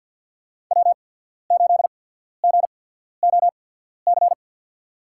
Morse code, heard as a keyed tone, about 52 wpm, spaced as for 14 wpm, 700 Hz. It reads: W8GOC
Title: USA Call Signs - 5 Characters - 8x Character Spacing - 50wpm